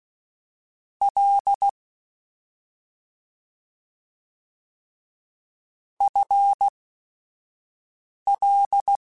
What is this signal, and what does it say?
Morse code, heard as a keyed tone, about 16 wpm, spaced as for 5 wpm, 800 Hz. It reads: L FL
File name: Short Runs / Short Runs-mp3s / ShortContrasts @5